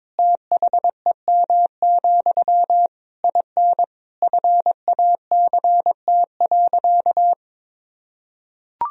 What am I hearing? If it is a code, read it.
Morse code, heard as a keyed tone, about 21 wpm, 700 Hz. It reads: THEM, IN FACT.
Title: Sentences from Top 1000 Words - Review Part 3 - 22wpm